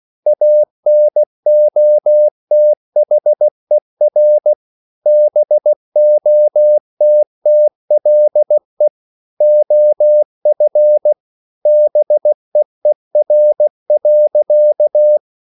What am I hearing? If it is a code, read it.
Morse code, heard as a keyed tone, about 16 wpm, 600 Hz. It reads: ANOTHER BOTTLE OF BEER.